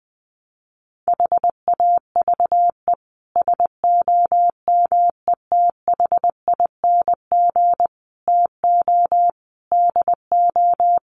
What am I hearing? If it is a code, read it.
Morse code, heard as a keyed tone, about 20 wpm, 700 Hz. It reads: HAVE SOMETHING TO DO
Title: Sets of 5 Words Encore 20wpm